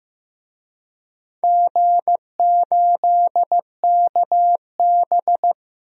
Morse code, heard as a keyed tone, about 15 wpm, 700 Hz. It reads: G8KB
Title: DXCC-100 Call Signs - 4 Characters - 15wpm